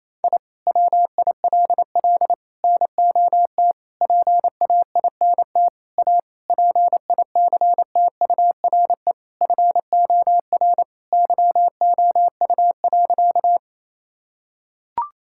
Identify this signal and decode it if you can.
Morse code, heard as a keyed tone, about 28 wpm, 700 Hz. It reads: I WILL NOT PAINT A PICTURE FOR YOU.
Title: Sentences from Top 1000 Words - Review Part 6 - 28wpm